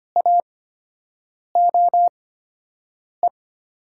Morse code, heard as a keyed tone, about 25 wpm, spaced as for 7 wpm, 700 Hz. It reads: AOE
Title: USA Call Signs - 8x Character Spacing - 25wpm